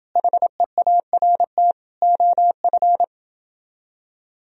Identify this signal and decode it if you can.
Morse code, heard as a keyed tone, about 30 wpm, 700 Hz. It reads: HEART OF